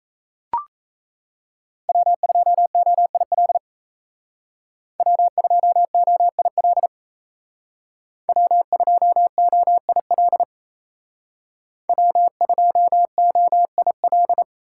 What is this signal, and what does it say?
Morse code, heard as a keyed tone, about 35 wpm, 700 Hz. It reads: W2OIL W2OIL W2OIL W2OIL
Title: Speed Racing - USA 1x3 Call Signs 28wpm